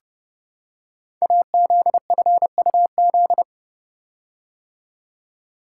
Morse code, heard as a keyed tone, about 29 wpm, 700 Hz. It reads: AZFUZ
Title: Code Groups - 5 Letters - 30wpm